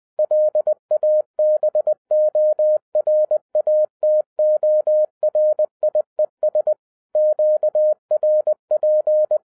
Morse code, heard as a keyed tone, about 20 wpm, 610 Hz. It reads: LABORATORIES QRP
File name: 2011-07-mp3 / n23 QRP OTAKU 20wpm